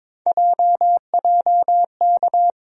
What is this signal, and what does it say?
Morse code, heard as a keyed tone, about 22 wpm, 700 Hz. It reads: JJK
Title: USA Call Sign Postfixes - 3 Letters 22wpm